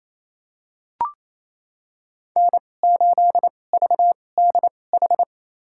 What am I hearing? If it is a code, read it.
Morse code, heard as a keyed tone, about 28 wpm, spaced as for 20 wpm, 700 Hz. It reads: N8VDH
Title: USA Call Signs - 5 Characters - 2x Character Spacing - 28wpm